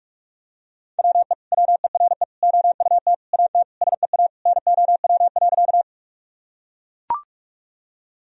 Morse code, heard as a keyed tone, about 41 wpm, 700 Hz. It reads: WE WERE OUT AT SEA NOW.
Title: Sentences from Top 1000 Words - Review Part 1 - 45wpm